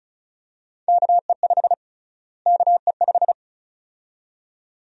Morse code, heard as a keyed tone, about 36 wpm, 700 Hz. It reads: KE5 KE5
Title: USA Call Sign Prefixes - Repeated 2x - 35wpm